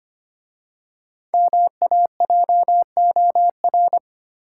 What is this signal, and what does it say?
Morse code, heard as a keyed tone, about 25 wpm, 700 Hz. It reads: MAJOR
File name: 5 Letter Words - First 3 Letters Given - 25wpm